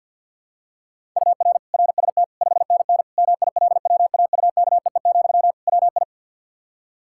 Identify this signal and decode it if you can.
Morse code, heard as a keyed tone, about 53 wpm, 700 Hz. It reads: UR RST 5NN MILWAUKEE, WI
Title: K1USN SST QSO Practice 50wpm